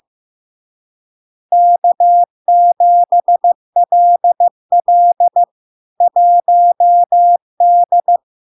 Morse code, heard as a keyed tone, about 15 wpm, 700 Hz. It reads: K7LL 1D
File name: LARK-FD-2022-Log-in-CW-15wpm-700-Hz